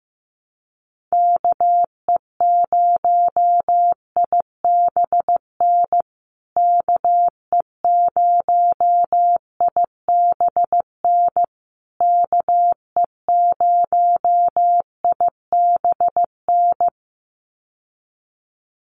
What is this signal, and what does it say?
Morse code, heard as a keyed tone, about 15 wpm, 700 Hz. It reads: KE0IBN KE0IBN KE0IBN
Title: USA Call Signs Repeated 3x 15wpm